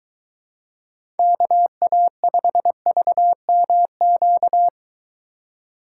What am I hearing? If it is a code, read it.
Morse code, heard as a keyed tone, about 23 wpm, 700 Hz. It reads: KA5VMQ